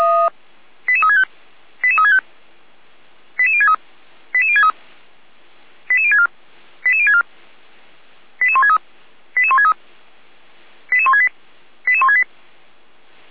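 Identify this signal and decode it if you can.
Five-tone selective call (ZVEI), heard as quick runs of five tones, tones of about 70 ms. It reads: 88365 88365 88063 88063 88064 88064 88163 88163 88168 88168
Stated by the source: ZVEI five-tone selective call received via SDR.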